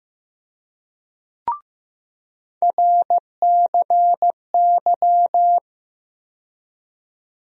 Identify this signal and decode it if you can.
Morse code, heard as a keyed tone, about 15 wpm, 700 Hz. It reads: RCY
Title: USA Call Sign Postfixes - 3 Letters 15wpm